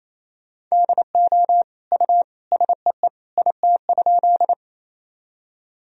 Morse code, heard as a keyed tone, about 28 wpm, 700 Hz. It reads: DO U SEE IT?
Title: Sentences from Top 100 Words and Abbreviations 28wpm